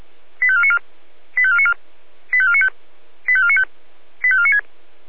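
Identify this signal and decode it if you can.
Five-tone selective call (ZVEI), heard as quick runs of five tones, tones of about 70 ms. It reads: 85494 85494 85495 85495 85496
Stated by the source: Received by SDR.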